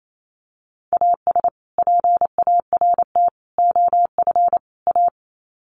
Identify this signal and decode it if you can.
Morse code, heard as a keyed tone, about 28 wpm, 700 Hz. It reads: AS PART OF A
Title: Sets of 4 Words 28wpm